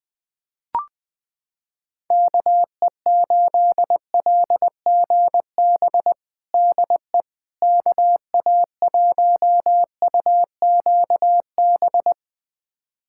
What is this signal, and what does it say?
Morse code, heard as a keyed tone, about 21 wpm, 700 Hz. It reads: KE8LGB DE KA1UQB
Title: USA Callsign DE Callsign - 6 Characters - 20wpm